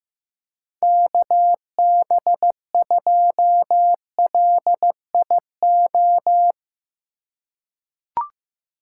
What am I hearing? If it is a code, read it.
Morse code, heard as a keyed tone, about 15 wpm, 700 Hz. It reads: KB2LIO
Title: USA Call Signs - 6 Characters - 8x Character Spacing - 15wpm